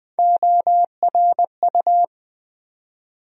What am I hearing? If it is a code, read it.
Morse code, heard as a keyed tone, about 20 wpm, 700 Hz. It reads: ORU
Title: USA Call Sign Postfixes - 3 Letters 20wpm